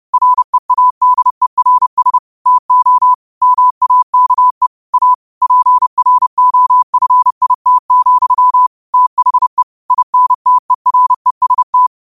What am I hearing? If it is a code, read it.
Morse code, heard as a keyed tone, about 30 wpm, 1000 Hz. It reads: READERS TO MAKE A PROFIT, THE INTEREST